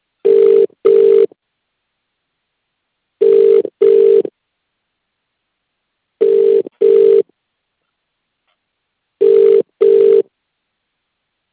Ringback tone (UK), 400 + 450 Hz.